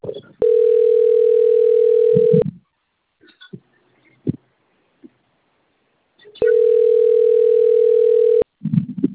Ringback tone (North America), 440 + 480 Hz.